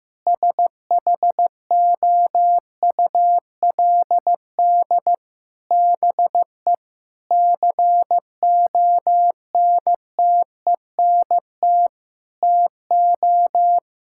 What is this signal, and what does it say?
Morse code, heard as a keyed tone, about 15 wpm, 700 Hz. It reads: SHOULD BE CONTENT TO
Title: Sentences from Top 1000 Words - Review Part 6 - 15wpm